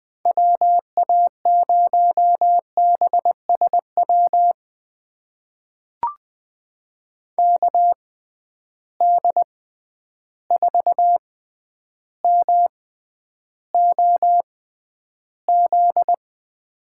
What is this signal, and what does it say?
Morse code, heard as a keyed tone, about 20 wpm, spaced as for 10 wpm, 700 Hz. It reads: WA0BSW KD4MOZ
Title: USA Call Signs - 6 Characters - 6x Character Spacing - 20wpm